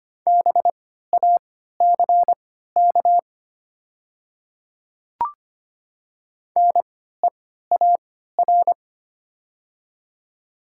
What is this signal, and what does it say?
Morse code, heard as a keyed tone, about 25 wpm, spaced as for 14 wpm, 700 Hz. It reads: BACK NEAR
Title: 3x Character Spacing - Top 200 Words - 25wpm